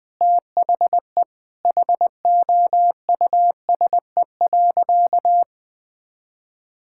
Morse code, heard as a keyed tone, about 20 wpm, 700 Hz. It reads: THE HOUSE.
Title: Sentences with 4-Word Phrases - Spoken Beginning - 20wpm